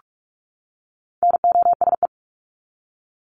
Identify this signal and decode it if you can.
Morse code, heard as a keyed tone, about 44 wpm, 700 Hz. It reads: NOSE
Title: Top 1501-1750 Words 45wpm